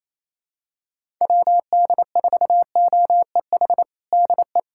Morse code, heard as a keyed tone, about 27 wpm, 700 Hz. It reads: WD4OEH DE
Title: USA Callsign DE Callsign - 6 Characters - 28wpm